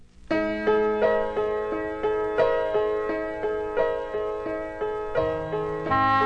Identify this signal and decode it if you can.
Busy tone (425 Hz, short), with noise.